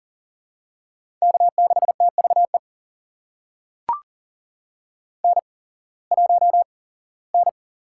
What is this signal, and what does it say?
Morse code, heard as a keyed tone, about 38 wpm, spaced as for 21 wpm, 700 Hz. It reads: K6TVE N1N
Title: USA Call Signs - 5 Characters - 8x Character Spacing - 40wpm